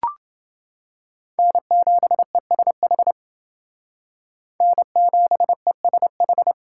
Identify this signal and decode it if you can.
Morse code, heard as a keyed tone, about 28 wpm, 700 Hz. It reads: N7ESH N7ESH